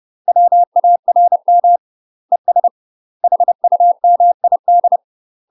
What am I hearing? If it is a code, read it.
Morse code, heard as a keyed tone, about 30 wpm, 700 Hz. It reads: WARM ES HUMID
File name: UK MASTER QSOs TRAINING / 30WPM / UK MASTER QSOs 0008